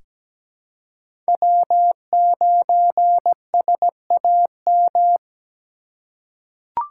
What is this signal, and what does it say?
Morse code, heard as a keyed tone, about 17 wpm, 700 Hz. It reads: W9SAM
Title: USA Call Signs - 5 Characters - 3x Character Spacing - 17wpm